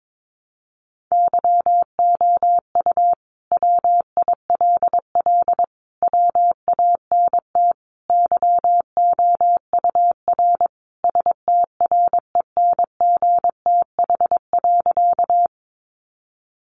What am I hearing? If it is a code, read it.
Morse code, heard as a keyed tone, about 22 wpm, 700 Hz. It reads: YOU WILL WANT YOUR STRENGTH.
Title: Sentences from Top 1300 Words 22wpm